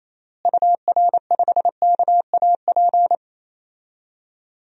Morse code, heard as a keyed tone, about 27 wpm, 700 Hz. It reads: UR5KAP
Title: DXCC-100 Call Signs - 6 Characters - Repeated 3x - 28wpm